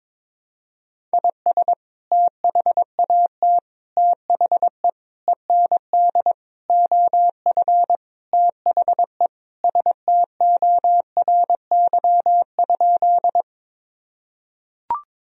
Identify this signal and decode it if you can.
Morse code, heard as a keyed tone, about 22 wpm, 700 Hz. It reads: IS THAT THE END OF THE STORY?
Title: Sentences from Top 1000 Words - Review Part 1 - 22wpm